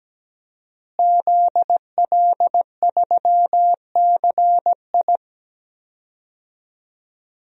Morse code, heard as a keyed tone, about 17 wpm, 700 Hz. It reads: ZL3CI
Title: DXCC-100 Call Signs - 5 Characters - Repeated 2x - 17wpm